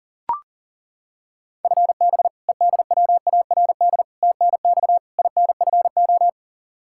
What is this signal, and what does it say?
Morse code, heard as a keyed tone, about 40 wpm, 700 Hz. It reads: FB EDWARD TNX INFO
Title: Practice QSO - Exchange 3 - More Info - 40wpm